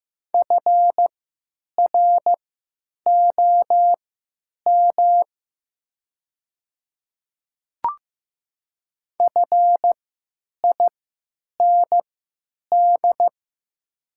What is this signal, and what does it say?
Morse code, heard as a keyed tone, about 15 wpm, spaced as for 9 wpm, 700 Hz. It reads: FROM FIND
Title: 3x Character Spacing - Top 100 Words - 15wpm